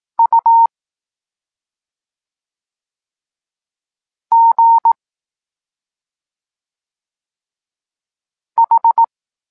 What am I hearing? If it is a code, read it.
Morse code, heard as a keyed tone, about 18 wpm, 920 Hz. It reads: U G H
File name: Tekst-23 4